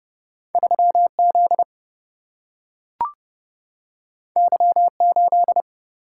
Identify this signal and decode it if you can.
Morse code, heard as a keyed tone, about 30 wpm, 700 Hz. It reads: 3Z Y8
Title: Single-Letter Single-Number 30wpm